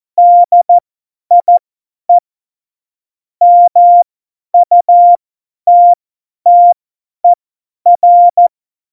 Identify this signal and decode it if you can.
Morse code, heard as a keyed tone, about 14 wpm, spaced as for 10 wpm, 700 Hz. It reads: DIE MUTTER
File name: Morsekurs / Witze / Witz 26. Dezember 2017, 50 F70